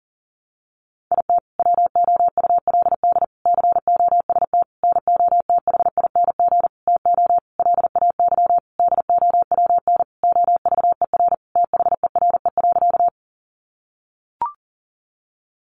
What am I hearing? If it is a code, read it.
Morse code, heard as a keyed tone, about 40 wpm, 700 Hz. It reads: IT WOULD COST NOTHING TO LAY DOWN OVER THERE.
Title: Sentences from Top 800 Words 40wpm